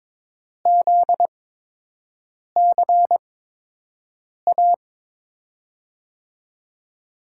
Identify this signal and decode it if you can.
Morse code, heard as a keyed tone, about 22 wpm, spaced as for 6 wpm, 700 Hz. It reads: ZCA
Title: USA Call Signs - 5 Characters - 8x Character Spacing - 22wpm